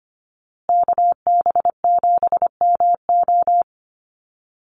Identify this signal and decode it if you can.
Morse code, heard as a keyed tone, about 25 wpm, 700 Hz. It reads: KB7MO
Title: USA Call Signs Repeated 2x 25wpm